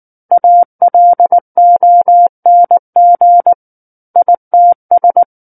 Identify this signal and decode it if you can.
Morse code, heard as a keyed tone, about 19 wpm, 700 Hz. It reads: ALONG ITS
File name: Book One / 10-Chapter 10 - 19 WPM